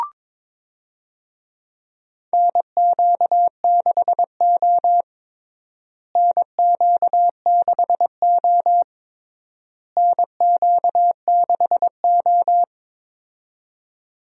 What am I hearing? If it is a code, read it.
Morse code, heard as a keyed tone, about 22 wpm, 700 Hz. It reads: NQ6O NQ6O NQ6O
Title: USA Call Signs - 4 Characters - Repeated 3x - 22wpm